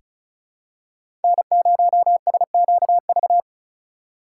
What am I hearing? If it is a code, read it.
Morse code, heard as a keyed tone, about 34 wpm, 700 Hz. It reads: N0SQV
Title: USA Call Signs - 5 Characters - 4x Character Spacing - 35wpm